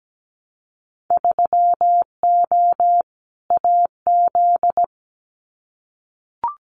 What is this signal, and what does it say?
Morse code, heard as a keyed tone, about 17 wpm, 700 Hz. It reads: 3O AZ
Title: Winter Field Day Exchanges 17wpm